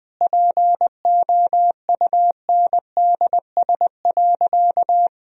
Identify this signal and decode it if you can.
Morse code, heard as a keyed tone, about 21 wpm, 700 Hz. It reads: POUNDS.
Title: Sentences from Top 1000 Words - Review Part 4 - 20wpm